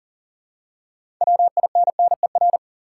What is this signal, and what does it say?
Morse code, heard as a keyed tone, about 40 wpm, 700 Hz. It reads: WINNER
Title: Six Letter Words - 40wpm